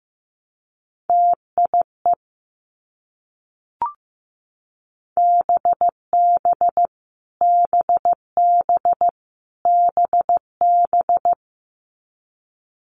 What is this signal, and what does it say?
Morse code, heard as a keyed tone, about 15 wpm, 700 Hz. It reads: TIE BB BB BB
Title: Multi-letter Phonemes Repeated 3x 15wpm